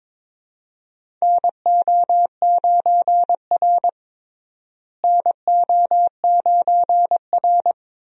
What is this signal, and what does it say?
Morse code, heard as a keyed tone, about 22 wpm, 700 Hz. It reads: NO9R NO9R
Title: USA Call Signs - 4 Characters - Repeated 2x - 22wpm